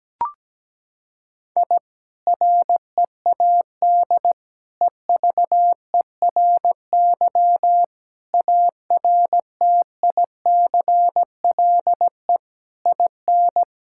Morse code, heard as a keyed tone, about 17 wpm, 700 Hz. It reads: I READ EVERY ARTICLE IN
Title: Sentences from Top 1000 Words - Review Part 5 - 17wpm